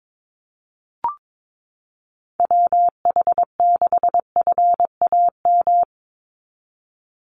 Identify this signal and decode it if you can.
Morse code, heard as a keyed tone, about 22 wpm, 700 Hz. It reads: WH6FAM
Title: USA Call Signs 22wpm